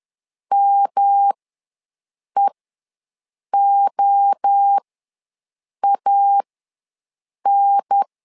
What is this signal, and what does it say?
Morse code, heard as a keyed tone, about 10 wpm, spaced as for 6 wpm, 800 Hz. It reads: MEOAN
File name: LEZIONI / 3 MWRO / 3 MWRO ws 10wpm 2310